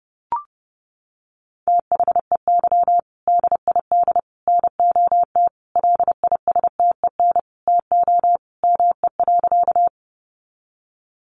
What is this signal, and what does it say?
Morse code, heard as a keyed tone, about 30 wpm, 700 Hz. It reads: THEY DID NOT LISTEN TO ME.
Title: Sentences from Top 600 Words 30wpm